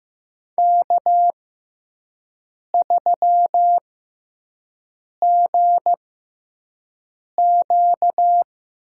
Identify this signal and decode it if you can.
Morse code, heard as a keyed tone, about 15 wpm, spaced as for 5 wpm, 700 Hz. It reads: K3GQ
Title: USA Call Signs - 6x Character Spacing - 15wpm